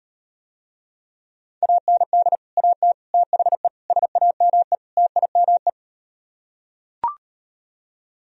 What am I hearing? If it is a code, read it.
Morse code, heard as a keyed tone, about 38 wpm, 700 Hz. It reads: AND AT THE SAME TIME